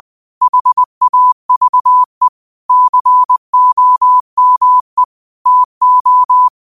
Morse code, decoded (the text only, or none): HAVE COME TO